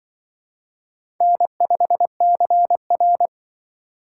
Morse code, heard as a keyed tone, about 25 wpm, 700 Hz. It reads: N5CR